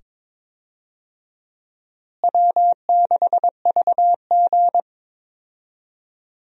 Morse code, heard as a keyed tone, about 22 wpm, 700 Hz. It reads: W6VG
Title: USA Call Signs - 4 Characters - Repeated 2x - 22wpm